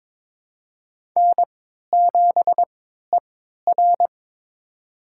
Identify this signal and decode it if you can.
Morse code, heard as a keyed tone, about 22 wpm, spaced as for 12 wpm, 700 Hz. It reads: N7ER
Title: USA Call Signs - 4 Characters - 3x Character Spacing - 22wpm